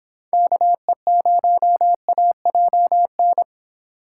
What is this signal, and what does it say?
Morse code, heard as a keyed tone, about 24 wpm, 700 Hz. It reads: KE0AJN